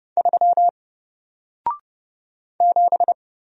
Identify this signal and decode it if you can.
Morse code, heard as a keyed tone, about 29 wpm, 700 Hz. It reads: 3 7
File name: Single-Letter-Number-30wpm